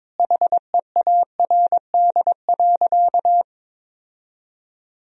Morse code, heard as a keyed tone, about 23 wpm, 700 Hz. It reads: HEARD.
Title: 2x Word-Spacing - Sentences from Top 400 Words - 22wpm